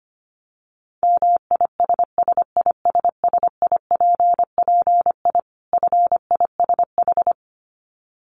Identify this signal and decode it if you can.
Morse code, heard as a keyed tone, about 25 wpm, 700 Hz. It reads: MISSISSIPPI FISH
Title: Dit Challenge 25wpm